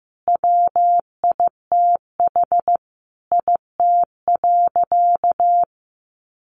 Morse code, heard as a keyed tone, about 15 wpm, 700 Hz. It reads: WITH IT.
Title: Sentences from Top 800 Words 15wpm